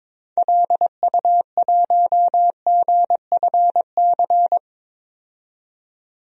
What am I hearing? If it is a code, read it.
Morse code, heard as a keyed tone, about 22 wpm, 700 Hz. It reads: LU1GFC
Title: DXCC-100 Call Signs - 6 Characters - 22wpm